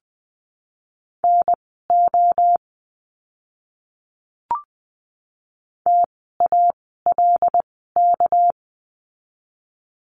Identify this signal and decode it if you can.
Morse code, heard as a keyed tone, about 19 wpm, spaced as for 14 wpm, 700 Hz. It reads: NO TALK
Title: 244 Lesson 39f - Letter z - Words 20wpm